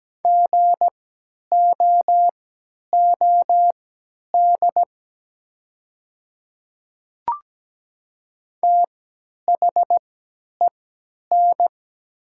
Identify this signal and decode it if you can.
Morse code, heard as a keyed tone, about 17 wpm, spaced as for 10 wpm, 700 Hz. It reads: GOOD THEN
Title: 3x Character Spacing - Top 200 Words - 17wpm